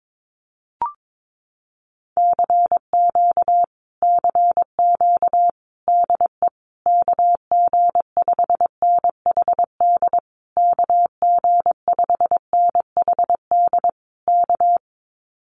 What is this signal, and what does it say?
Morse code, heard as a keyed tone, about 22 wpm, 700 Hz. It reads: CQ CQ DE KG5NHD KG5NHD K
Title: Common QSO Phrases Encore 22wpm